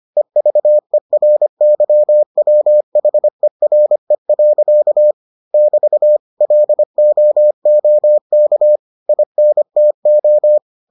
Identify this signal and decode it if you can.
Morse code, heard as a keyed tone, about 25 wpm, 600 Hz. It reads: EVERYWHERE. <BT> LOOK INTO